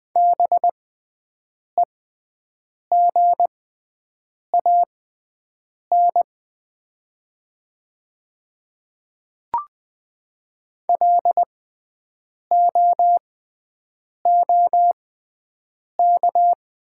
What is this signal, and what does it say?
Morse code, heard as a keyed tone, about 20 wpm, spaced as for 7 wpm, 700 Hz. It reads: BEGAN LOOK
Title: 6x Character Spacing - Top 300 Words - 20wpm